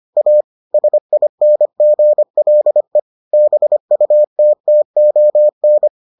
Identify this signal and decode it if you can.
Morse code, heard as a keyed tone, about 25 wpm, 600 Hz. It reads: A SINGLE BUTTON